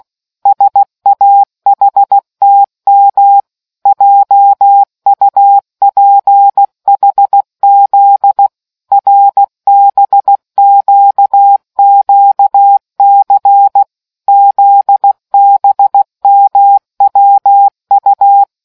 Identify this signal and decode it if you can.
Morse code, heard as a keyed tone, about 16 wpm, 800 Hz. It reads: SAHTM JUPHZ RBQQC ZBMWU